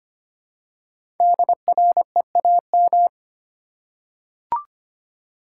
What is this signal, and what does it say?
Morse code, heard as a keyed tone, about 25 wpm, 700 Hz. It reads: DREAM
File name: 5 Letter Words - First 3 Letters Given - 25wpm